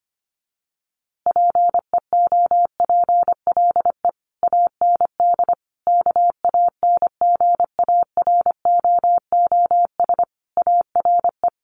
Morse code, heard as a keyed tone, about 25 wpm, 700 Hz. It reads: PEOPLE AND KANGAROOS ARE
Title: Easy Phrases 25wpm